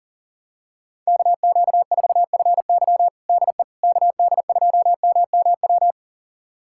Morse code, heard as a keyed tone, about 40 wpm, 700 Hz. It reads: KQ4FY DE KD2MMW
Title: USA Callsign DE Callsign 40wpm